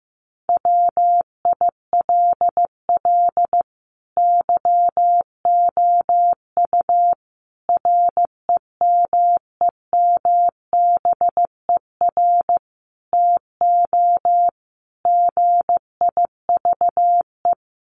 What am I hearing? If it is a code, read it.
Morse code, heard as a keyed tone, about 15 wpm, 700 Hz. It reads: WILL YOU REMEMBER TO GIVE
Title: Sentences from Top 1100 Words 15wpm